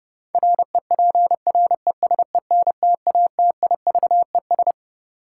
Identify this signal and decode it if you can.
Morse code, heard as a keyed tone, about 30 wpm, 700 Hz. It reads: REPRESENTATIVES
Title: Sets of 5 Words Encore - New Word Drill - 30wpm